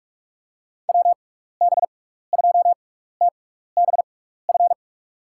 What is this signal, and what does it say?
Morse code, heard as a keyed tone, about 41 wpm, spaced as for 15 wpm, 700 Hz. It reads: WB2TBF
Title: USA Call Signs - 6 Characters - 6x Character Spacing - 45wpm